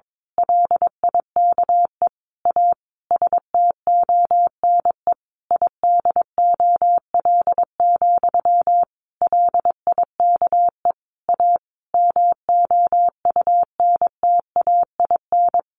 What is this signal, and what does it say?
Morse code, heard as a keyed tone, about 22 wpm, 700 Hz. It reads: LIKE A STONE IDOL, LIKE A MOUNTAIN
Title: A Sound Of Thunder 22wpm